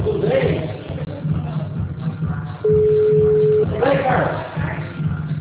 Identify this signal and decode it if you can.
Ringback tone (Europe, 425 Hz), with speech.